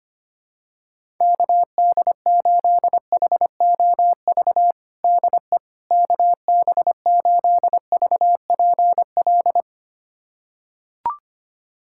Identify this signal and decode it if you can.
Morse code, heard as a keyed tone, about 25 wpm, 700 Hz. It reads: KD8HOV DE KB8VPL
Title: USA Callsign DE Callsign - 6 Characters - 25wpm